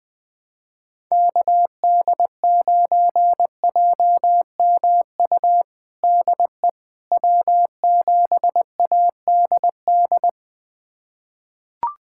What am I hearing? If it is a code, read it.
Morse code, heard as a keyed tone, about 20 wpm, 700 Hz. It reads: KD9JMU DE W7ADD
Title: USA Callsign DE Callsign 20wpm